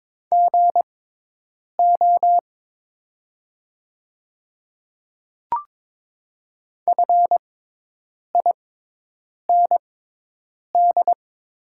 Morse code, heard as a keyed tone, about 22 wpm, spaced as for 8 wpm, 700 Hz. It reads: GO FIND
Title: 6x Character Spacing - Top 100 Words - 22wpm